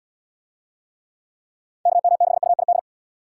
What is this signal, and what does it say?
Morse code, heard as a keyed tone, about 78 wpm, 700 Hz. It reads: KM6UID